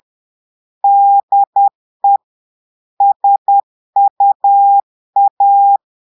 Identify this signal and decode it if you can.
Morse code, heard as a keyed tone, about 10 wpm, 800 Hz. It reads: DE SUA
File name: Portugais - texte en clair 1 - vitesse - 10 mots minute